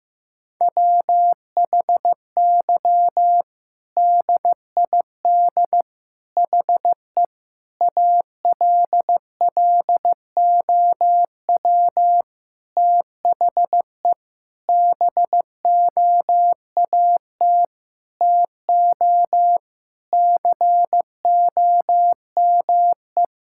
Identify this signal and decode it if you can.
Morse code, heard as a keyed tone, about 15 wpm, 700 Hz. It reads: WHY DID HE ALLOW THE BOAT TO COME
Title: Sentences from Top 1000 Words - Review Part 5 - 15wpm